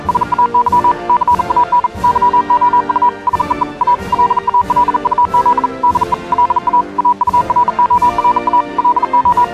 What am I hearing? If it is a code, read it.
Morse code, heard as a keyed tone, about 37 wpm, 980 Hz. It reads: SAY CAN YOU SEE A LA LUZ DE LA AURORA LO